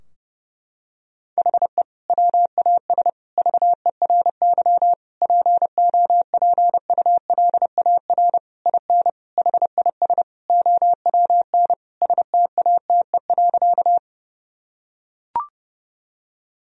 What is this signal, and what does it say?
Morse code, heard as a keyed tone, about 32 wpm, 700 Hz. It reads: HE WAS VERY POPULAR IN HIS OWN STATE.
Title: Sentences from Top 1000 Words - Review Part 3 - 30wpm